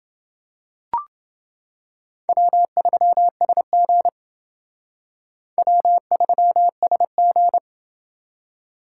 Morse code, heard as a keyed tone, about 28 wpm, 700 Hz. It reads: W3SG W3SG